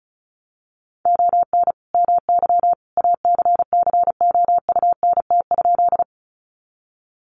Morse code, heard as a keyed tone, about 34 wpm, 700 Hz. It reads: ON MY ACCOUNT?
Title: Sentences from Top 900 Words 35wpm